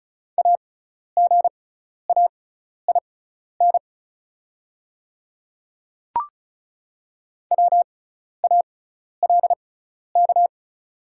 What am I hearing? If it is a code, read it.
Morse code, heard as a keyed tone, about 36 wpm, spaced as for 12 wpm, 700 Hz. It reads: AGAIN WALK
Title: 6x Character Spacing - Top 300 Words - 35wpm